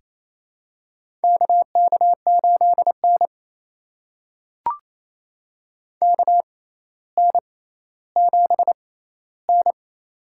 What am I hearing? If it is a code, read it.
Morse code, heard as a keyed tone, about 28 wpm, spaced as for 14 wpm, 700 Hz. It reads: KK8N KN7N
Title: USA Call Signs - 4 Characters - 6x Character Spacing - 28wpm